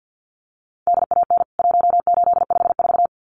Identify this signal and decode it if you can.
Morse code, heard as a keyed tone, about 50 wpm, 700 Hz. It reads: DAN 1854
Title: CWT Exchange 50wpm